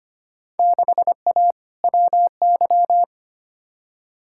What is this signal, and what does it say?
Morse code, heard as a keyed tone, about 25 wpm, 700 Hz. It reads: 6A WY
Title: Field Day Exchanges Encore - 25wpm